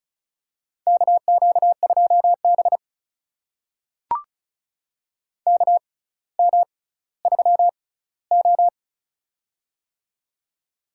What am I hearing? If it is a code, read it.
Morse code, heard as a keyed tone, about 36 wpm, spaced as for 18 wpm, 700 Hz. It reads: KQ2B KM3O
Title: USA Call Signs - 4 Characters - 6x Character Spacing - 35wpm